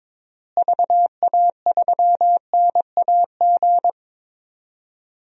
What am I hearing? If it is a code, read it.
Morse code, heard as a keyed tone, about 21 wpm, 700 Hz. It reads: VA3NAG
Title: DXCC-100 Call Signs - 6 Characters - Repeated 3x - 22wpm